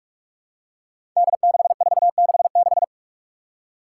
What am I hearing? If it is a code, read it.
Morse code, heard as a keyed tone, about 46 wpm, 700 Hz. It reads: D6466
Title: Five Challenging Letter-Number Combinations 45wpm